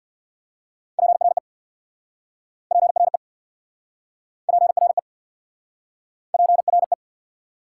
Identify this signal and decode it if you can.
Morse code, heard as a keyed tone, about 67 wpm, 700 Hz. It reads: PRE PRE PRE PRE